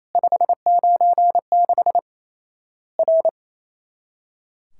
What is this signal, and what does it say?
Morse code, heard as a keyed tone, about 28 wpm, 700 Hz. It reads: 596 R
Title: Worked All Europe DX Contest - QTC Exchange - 28wpm